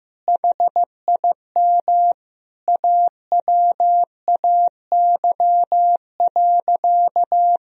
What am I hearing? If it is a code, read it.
Morse code, heard as a keyed tone, about 15 wpm, 700 Hz. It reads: HIM AWAY.
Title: Sentences from Top 1000 Words - Review Part 2 - 15wpm